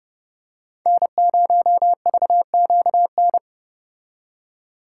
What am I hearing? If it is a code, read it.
Morse code, heard as a keyed tone, about 29 wpm, 700 Hz. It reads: N0VQN